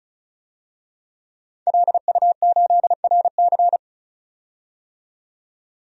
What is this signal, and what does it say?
Morse code, heard as a keyed tone, about 34 wpm, 700 Hz. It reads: LU8RC
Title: DXCC-100 Call Signs - 5 Characters - 35wpm